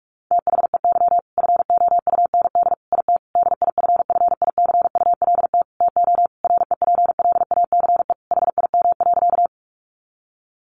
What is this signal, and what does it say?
Morse code, heard as a keyed tone, about 44 wpm, 700 Hz. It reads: THEY FOUND IT DIFFICULT TO REPLACE HIM.
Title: Sentences from Top 1400 Words 45wpm